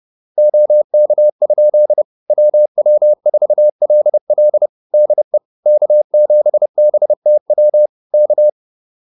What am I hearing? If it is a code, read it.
Morse code, heard as a keyed tone, about 30 wpm, 600 Hz. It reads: OK? WW4LL DE K7BTW K